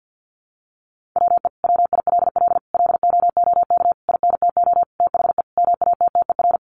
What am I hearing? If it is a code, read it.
Morse code, heard as a keyed tone, about 50 wpm, 700 Hz. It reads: WE WILL LOOK INTO THE MATTER
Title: Sentences from Top 1400 Words 50wpm